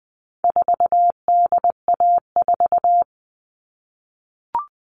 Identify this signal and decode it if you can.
Morse code, heard as a keyed tone, about 20 wpm, 700 Hz. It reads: 4DA4
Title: All Cut Numbers - 4-Digit 20wpm